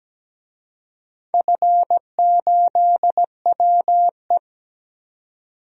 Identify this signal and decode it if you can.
Morse code, heard as a keyed tone, about 17 wpm, 700 Hz. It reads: F8WE
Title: DXCC-100 Call Signs - 4 Characters - 17wpm